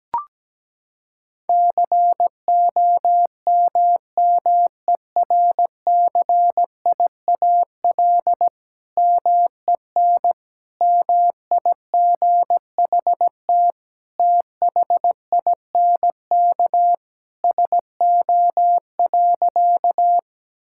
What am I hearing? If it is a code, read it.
Morse code, heard as a keyed tone, about 17 wpm, 700 Hz. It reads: COMMERCIAL MEN MIGHT THINK SO.
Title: Sentences from Top 1000 Words - Review Part 5 - 17wpm